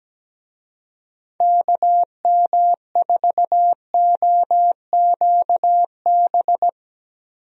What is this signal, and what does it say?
Morse code, heard as a keyed tone, about 17 wpm, 700 Hz. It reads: KM4OQB
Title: USA Call Signs - 6 Characters - Repeated 2x - 17wpm